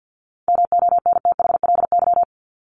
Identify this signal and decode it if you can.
Morse code, heard as a keyed tone, about 48 wpm, 700 Hz. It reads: MONTHLY
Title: LY Suffix - Top 100 Words with Base Words 50wpm